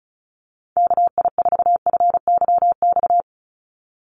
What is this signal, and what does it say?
Morse code, heard as a keyed tone, about 34 wpm, 700 Hz. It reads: KI4FYX
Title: USA Call Signs 35wpm